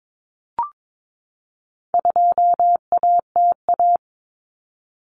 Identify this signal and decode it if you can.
Morse code, heard as a keyed tone, about 22 wpm, 700 Hz. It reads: 2ATA
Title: Common Cut Numbers - 4-Digit 22wpm